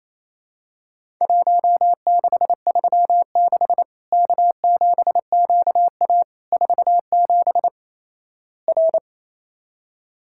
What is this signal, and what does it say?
Morse code, heard as a keyed tone, about 27 wpm, 700 Hz. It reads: 1636 K7QA 47 R
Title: Worked All Europe DX Contest - QTC Exchange - 28wpm